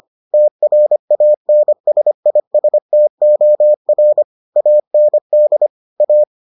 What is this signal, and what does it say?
Morse code, heard as a keyed tone, about 25 wpm, 600 Hz. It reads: TRANSISTOR AND A